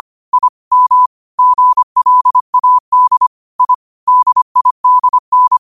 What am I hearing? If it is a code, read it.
Morse code, heard as a keyed tone, about 25 wpm, 1000 Hz. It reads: I M GLAD I DIDN
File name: 25wpm